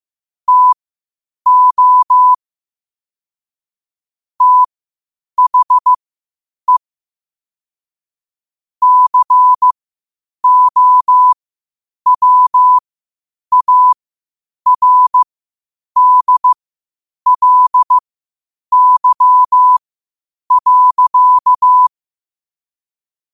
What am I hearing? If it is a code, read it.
Morse code, heard as a keyed tone, about 15 wpm, spaced as for 8 wpm, 1000 Hz. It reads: TO THE COWARDLY.
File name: qotdfiles / 8wpm